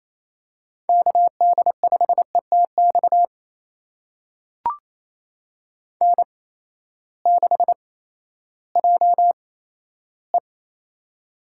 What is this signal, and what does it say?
Morse code, heard as a keyed tone, about 28 wpm, spaced as for 14 wpm, 700 Hz. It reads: KD5ETX N6JE
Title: USA Call Signs - 8x Character Spacing - 28wpm